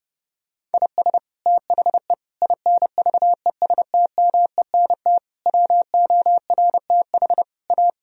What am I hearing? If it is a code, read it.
Morse code, heard as a keyed tone, about 32 wpm, 700 Hz. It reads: IS THE INVESTMENT WORTH A
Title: Sentences from Top 1000 Words - Review Part 4 - 30wpm